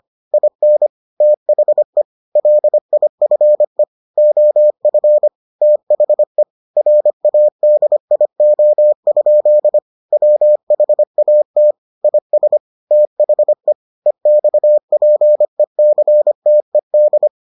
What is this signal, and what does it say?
Morse code, heard as a keyed tone, about 25 wpm, 600 Hz. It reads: IN THE LIFE OF THE RADIO? WHAT IS THE EXPECTED